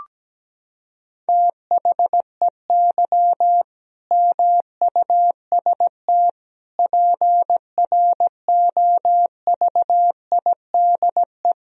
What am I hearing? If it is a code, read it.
Morse code, heard as a keyed tone, about 17 wpm, 700 Hz. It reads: THEY MUST PROVIDE
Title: Sentences from Top 1000 Words - Review Part 3 - 17wpm